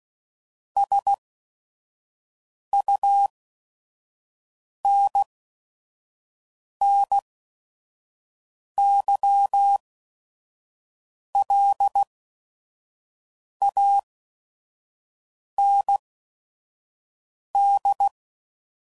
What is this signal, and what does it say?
Morse code, heard as a keyed tone, about 16 wpm, spaced as for 5 wpm, 800 Hz. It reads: SUNNYLAND
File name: CW Code Groups @5WPM / fromSM's / Cities